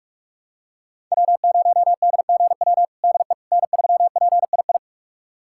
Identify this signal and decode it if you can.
Morse code, heard as a keyed tone, about 41 wpm, 700 Hz. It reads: W0DGW DE N3PII
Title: USA Callsign DE Callsign - 5 Characters - 45wpm